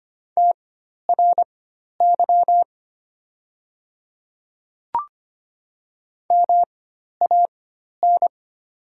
Morse code, heard as a keyed tone, about 25 wpm, spaced as for 12 wpm, 700 Hz. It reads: TRY MAN
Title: 4x Character Spacing - Top 200 Words - 25wpm